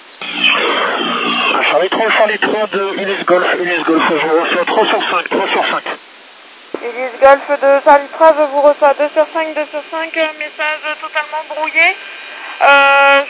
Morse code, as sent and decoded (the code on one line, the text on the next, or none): none